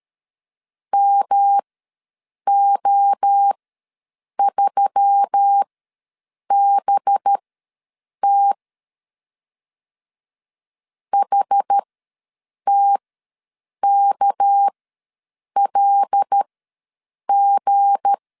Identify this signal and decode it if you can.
Morse code, heard as a keyed tone, about 13 wpm, spaced as for 7 wpm, 800 Hz. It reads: MO3BT HTKLG
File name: LEZIONI / 7 GF9Z / 7 G ws 12wpm 3140